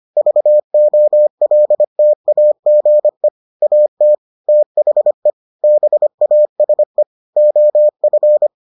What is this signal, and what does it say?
Morse code, heard as a keyed tone, about 25 wpm, 600 Hz. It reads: VOLTAGE AT THE BASE OF